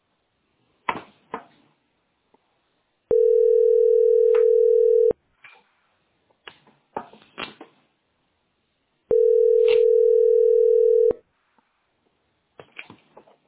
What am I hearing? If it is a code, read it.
Ringback tone (North America), 440 + 480 Hz.